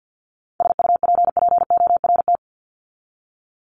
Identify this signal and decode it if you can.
Morse code, heard as a keyed tone, about 50 wpm, 700 Hz. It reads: SUPPORT
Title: Top 751-1000 Words 50wpm